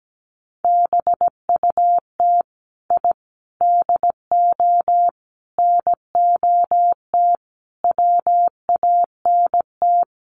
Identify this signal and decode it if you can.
Morse code, heard as a keyed tone, about 17 wpm, 700 Hz. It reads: BUT I DO NOT WANT
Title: Sets of 5 Words 17wpm